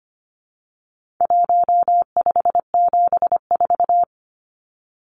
Morse code, heard as a keyed tone, about 25 wpm, 700 Hz. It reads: 1574
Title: Numbers - 4-Digit 25wpm